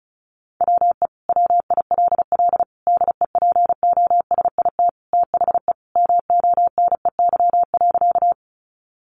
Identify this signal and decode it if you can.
Morse code, heard as a keyed tone, about 34 wpm, 700 Hz. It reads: WE WILL DEPOSIT THE MONEY.
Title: Sentences from Top 1300 Words 35wpm